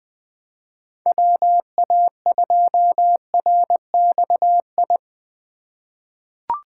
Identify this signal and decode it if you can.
Morse code, heard as a keyed tone, about 20 wpm, 700 Hz. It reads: WA2RXI
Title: USA Call Signs - 6 Characters - 3x Character Spacing - 20wpm